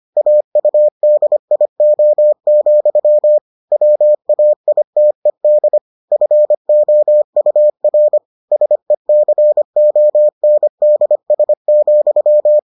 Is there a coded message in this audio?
Yes: Morse code from 0.2 to 12.6 s, heard as a keyed tone, about 25 wpm, 600 Hz. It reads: AUDIO, WAITED FOUR SECONDS,